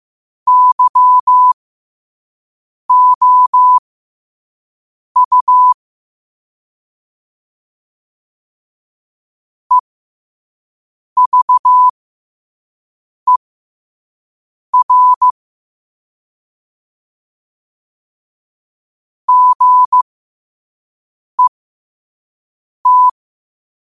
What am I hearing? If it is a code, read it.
Morse code, heard as a keyed tone, about 15 wpm, spaced as for 5 wpm, 1000 Hz. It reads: YOU EVER GET
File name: qotdfiles / 5wpm